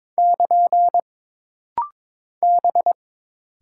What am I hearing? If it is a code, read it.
Morse code, heard as a keyed tone, about 22 wpm, 700 Hz. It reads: <KN> B
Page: This rapid-fire timing is geared to push your ICR (Instant Character Recognition) skills.